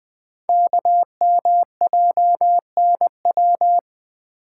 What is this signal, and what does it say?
Morse code, heard as a keyed tone, about 20 wpm, 700 Hz. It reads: KMJNW
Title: Code Groups - 5 Letters - 20wpm